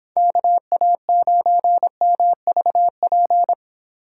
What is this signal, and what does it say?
Morse code, heard as a keyed tone, about 26 wpm, 700 Hz. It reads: KA9MVP